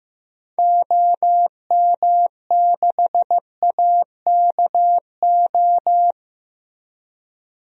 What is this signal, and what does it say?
Morse code, heard as a keyed tone, about 15 wpm, 700 Hz. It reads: OM6AKO
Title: DXCC-100 Call Signs - 6 Characters - 15wpm